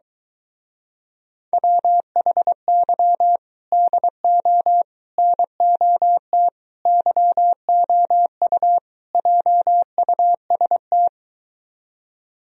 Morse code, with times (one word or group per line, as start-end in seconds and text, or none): 1.5-3.4 WHY
3.7-4.8 DO
5.2-6.5 NOT
6.9-8.8 YOU
9.1-11.1 JUST